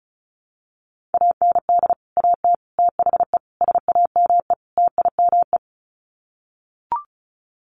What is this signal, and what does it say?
Morse code, heard as a keyed tone, about 34 wpm, 700 Hz. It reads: AND AT THE SAME TIME
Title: Sets of 5 Words 35wpm